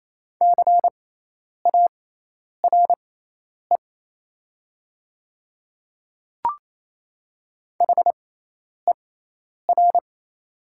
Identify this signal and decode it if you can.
Morse code, heard as a keyed tone, about 28 wpm, spaced as for 10 wpm, 700 Hz. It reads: CARE HER
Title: 6x Character Spacing - Top 300 Words - 28wpm